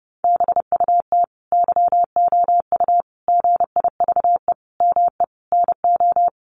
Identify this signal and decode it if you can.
Morse code, heard as a keyed tone, about 30 wpm, 700 Hz. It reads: BUT YOU GIVE ME NO
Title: Sentences from Top 1500 Words 30wpm